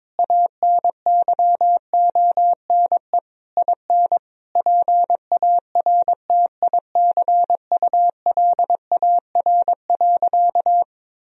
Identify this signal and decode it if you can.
Morse code, heard as a keyed tone, about 21 wpm, 700 Hz. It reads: ANYONE IN PARTICULAR.
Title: Sentences from Top 1000 Words - Review Part 4 - 22wpm